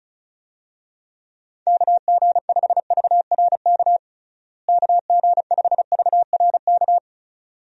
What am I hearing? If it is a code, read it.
Morse code, heard as a keyed tone, about 34 wpm, 700 Hz. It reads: KG5VRK KG5VRK
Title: USA Call Signs - 6 Characters - Repeated 2x - 35wpm